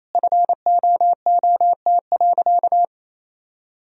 Morse code, heard as a keyed tone, about 28 wpm, 700 Hz. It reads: FOOT.
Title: Sentences from Top 1000 Words - Review Part 3 - 28wpm